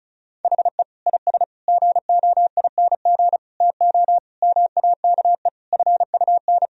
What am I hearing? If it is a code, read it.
Morse code, heard as a keyed tone, about 36 wpm, 700 Hz. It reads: HE IS GOING TO MAKE FUN
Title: Sentences from Top 1000 Words - Review Part 4 - 35wpm